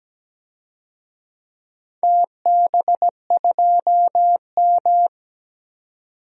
Morse code, heard as a keyed tone, about 17 wpm, 700 Hz. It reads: TB2M
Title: DXCC-100 Call Signs - 4 Characters - 17wpm